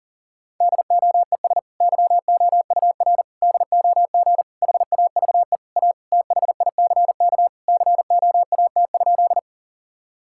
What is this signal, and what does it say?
Morse code, heard as a keyed tone, about 40 wpm, 700 Hz. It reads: DOES YOUR DOG HAVE A THICK COAT?
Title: Sentences from Top 1600 Words 40wpm